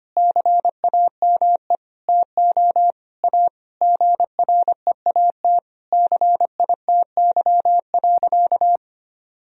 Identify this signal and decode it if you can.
Morse code, heard as a keyed tone, about 25 wpm, 700 Hz. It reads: CAME TO A GREAT CITY.
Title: Sentences from Top 1000 Words - Review Part 1 - 25wpm